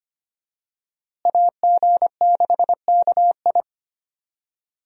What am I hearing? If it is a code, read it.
Morse code, heard as a keyed tone, about 25 wpm, 700 Hz. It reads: AG6KI
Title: USA Call Signs - 5 Characters - Repeated 3x - 25wpm